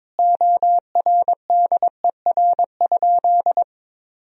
Morse code, heard as a keyed tone, about 22 wpm, 700 Hz. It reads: ORDER?
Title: Sentences from Top 1000 Words - Review Part 2 - 22wpm